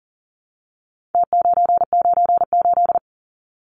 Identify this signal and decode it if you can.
Morse code, heard as a keyed tone, about 40 wpm, 700 Hz. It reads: T998
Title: Numbers - Contest Style - 4-Digit 40wpm